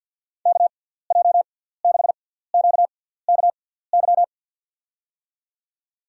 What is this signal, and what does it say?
Morse code, heard as a keyed tone, about 52 wpm, spaced as for 17 wpm, 700 Hz. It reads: KJ6QKY
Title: USA Call Signs - 6 Characters - 6x Character Spacing - 50wpm